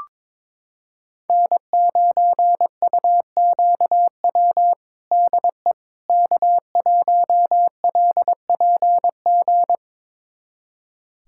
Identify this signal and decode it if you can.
Morse code, heard as a keyed tone, about 22 wpm, 700 Hz. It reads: N9UQW DE K1LPG
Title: USA Callsign DE Callsign - 5 Characters - 22wpm